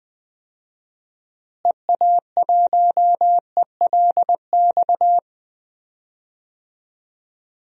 Morse code, heard as a keyed tone, about 20 wpm, 700 Hz. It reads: EA1ELX
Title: DXCC-100 Call Signs - 6 Characters - Repeated 2x - 20wpm